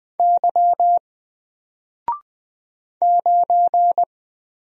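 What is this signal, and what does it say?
Morse code, heard as a keyed tone, about 20 wpm, 700 Hz. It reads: Y 9
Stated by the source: Single letter and number Morse drill at 20 wpm, from N4LCD.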